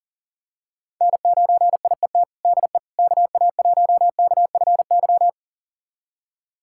Morse code, heard as a keyed tone, about 40 wpm, 700 Hz. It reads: N9IET DE KA1KFY
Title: USA Callsign DE Callsign 40wpm